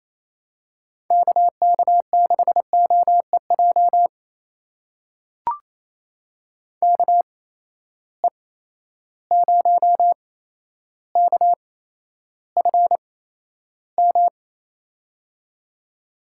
Morse code, heard as a keyed tone, about 28 wpm, spaced as for 12 wpm, 700 Hz. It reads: KK6OEJ KE0KFM
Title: USA Call Signs - 6 Characters - 8x Character Spacing - 28wpm